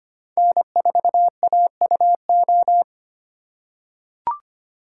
Morse code, heard as a keyed tone, about 25 wpm, 700 Hz. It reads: N4AUO
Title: USA Call Signs - 4x Character Spacing - 25wpm